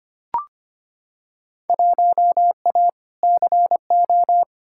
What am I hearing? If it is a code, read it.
Morse code, heard as a keyed tone, about 25 wpm, 700 Hz. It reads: 1A CO
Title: Field Day Exchanges - 25wpm